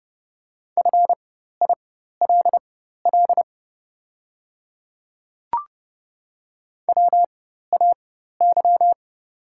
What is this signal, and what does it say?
Morse code, heard as a keyed tone, about 30 wpm, spaced as for 14 wpm, 700 Hz. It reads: FILL WAY
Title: 4x Character Spacing - Top 500 Words - 30wpm